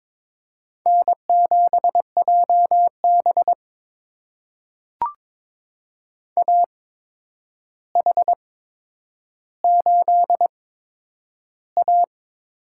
Morse code, heard as a keyed tone, about 22 wpm, spaced as for 9 wpm, 700 Hz. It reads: N7JB AH8A
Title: USA Call Signs - 4 Characters - 8x Character Spacing - 22wpm